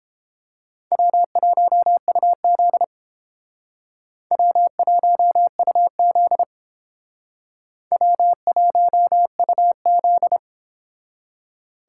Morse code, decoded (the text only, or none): W1UZ W1UZ W1UZ